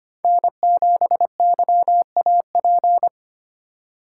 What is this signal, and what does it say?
Morse code, heard as a keyed tone, about 25 wpm, 700 Hz. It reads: N7YAP
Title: USA Call Signs - 5 Characters - 4x Character Spacing - 25wpm